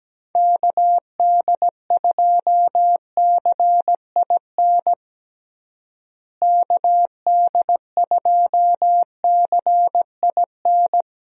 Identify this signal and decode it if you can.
Morse code, heard as a keyed tone, about 17 wpm, 700 Hz. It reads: KD2CIN KD2CIN
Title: USA Call Signs - 6 Characters - Repeated 3x - 17wpm